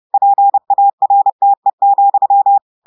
Morse code, heard as a keyed tone, about 30 wpm, 800 Hz. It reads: PARTE,
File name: audios / portugais / portugais texte en clair 1 vitesse 30 mots minutes